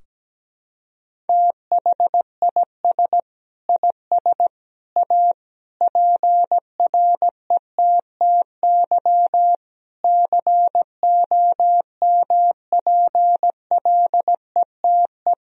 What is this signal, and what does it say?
Morse code, heard as a keyed tone, about 17 wpm, 700 Hz. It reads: THIS IS A PRETTY COMPLETE
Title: Sentences from Top 1000 Words - Review Part 2 - 17wpm